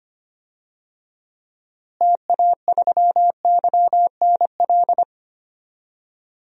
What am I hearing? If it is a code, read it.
Morse code, heard as a keyed tone, about 25 wpm, 700 Hz. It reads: TA3YNL
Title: DXCC-100 Call Signs - 6 Characters - Repeated 2x - 25wpm